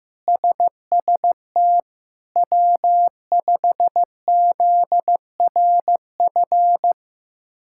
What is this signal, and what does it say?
Morse code, heard as a keyed tone, about 16 wpm, 700 Hz. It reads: SST W5ZRF
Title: K1USN SST QSO Practice - 15wpm